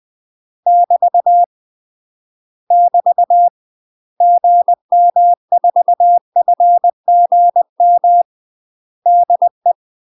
Morse code, heard as a keyed tone, about 20 wpm, 700 Hz. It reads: <BT> <BT> GM4FGM DE
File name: UK MASTER QSOs TRAINING / 20WPM / UK MASTER QSOs 0050